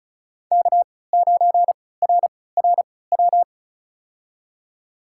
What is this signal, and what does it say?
Morse code, heard as a keyed tone, about 36 wpm, spaced as for 20 wpm, 700 Hz. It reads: K9RRW
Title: USA Call Signs - 5 Characters - 3x Character Spacing - 35wpm